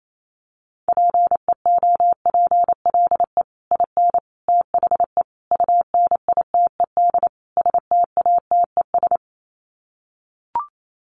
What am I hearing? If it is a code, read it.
Morse code, heard as a keyed tone, about 28 wpm, 700 Hz. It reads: PEOPLE IN THE UNITED STATES
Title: Sets of 5 Words Encore 28wpm